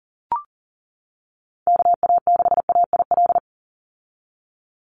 Morse code, heard as a keyed tone, about 40 wpm, 700 Hz. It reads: KA6AIL
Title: Call Signs - 40wpm